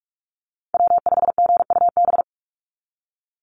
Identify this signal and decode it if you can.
Morse code, heard as a keyed tone, about 46 wpm, 690 Hz. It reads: W5GUB
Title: USA Call Signs 45wpm